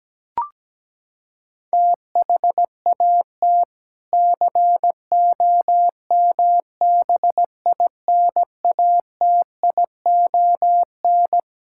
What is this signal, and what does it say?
Morse code, heard as a keyed tone, about 17 wpm, 700 Hz. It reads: THAT COMBINATION
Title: Sentences from Top 1000 Words - Review Part 6 - 17wpm